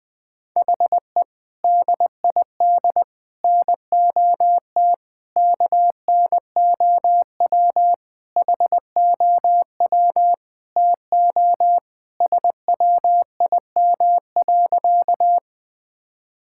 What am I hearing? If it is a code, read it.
Morse code, heard as a keyed tone, about 20 wpm, 700 Hz. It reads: HE DID NOT KNOW HOW TO SWIM.
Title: Sentences with 5-Word Phrases - 20wpm